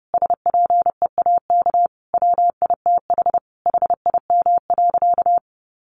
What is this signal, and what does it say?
Morse code, heard as a keyed tone, about 30 wpm, 700 Hz. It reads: SPEAK WITH HIM.
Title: Sentences from Top 1100 Words 30wpm